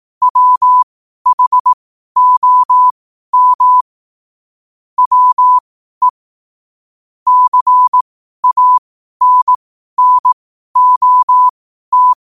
Morse code, heard as a keyed tone, about 18 wpm, spaced as for 12 wpm, 1000 Hz. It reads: WHOM WE CANNOT
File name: qotdfiles / 12wpm